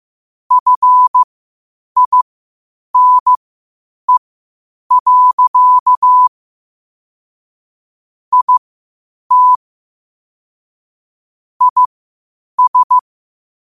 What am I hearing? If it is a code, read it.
Morse code, heard as a keyed tone, about 15 wpm, spaced as for 8 wpm, 1000 Hz. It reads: FINE. IT IS
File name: qotdfiles / 8wpm